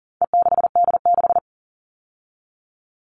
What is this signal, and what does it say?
Morse code, heard as a keyed tone, about 40 wpm, 700 Hz. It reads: E6D6
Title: All Cut Numbers - 4-Digit 40wpm